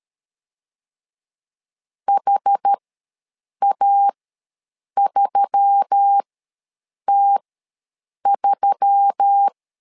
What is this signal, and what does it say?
Morse code, heard as a keyed tone, about 13 wpm, spaced as for 7 wpm, 800 Hz. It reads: HA3T3
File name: LEZIONI / 4 Q1S3 ws 12wpm 2038